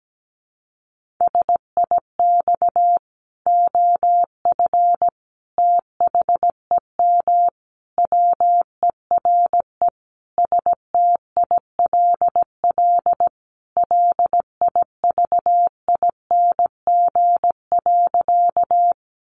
Morse code, heard as a keyed tone, about 17 wpm, 700 Hz. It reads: SIX OF THEM WERE STILL LIVING.
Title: Sentences from Top 1100 Words 17wpm